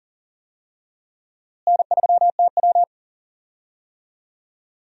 Morse code, heard as a keyed tone, about 40 wpm, 700 Hz. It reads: N3TW
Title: USA Call Signs - 4 Characters - Repeated 3x - 40wpm